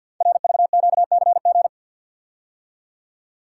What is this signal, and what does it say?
Morse code, heard as a keyed tone, about 52 wpm, 700 Hz. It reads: RVQCG
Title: Code Groups - 5 Letters - 50wpm